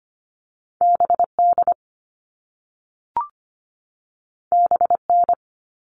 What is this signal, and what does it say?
Morse code, heard as a keyed tone, about 25 wpm, 700 Hz. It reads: B D B N
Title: All Cut Numbers - 2-Digit 25wpm